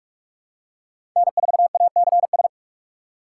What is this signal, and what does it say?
Morse code, heard as a keyed tone, about 43 wpm, 700 Hz. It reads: N4ACS
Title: USA Call Signs - 6x Character Spacing - 45wpm